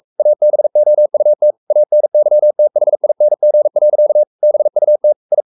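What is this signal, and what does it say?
Morse code, heard as a keyed tone, about 44 wpm, 600 Hz. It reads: ABOUT ANYTHING. BUT I
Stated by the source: Morse code at 43 wpm sending Ken Follett's novel Never in English, from CW med Gnister.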